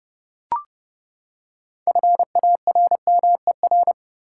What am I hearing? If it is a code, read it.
Morse code, heard as a keyed tone, about 29 wpm, 700 Hz. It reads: FARMER
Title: Six Letter Words - 30wpm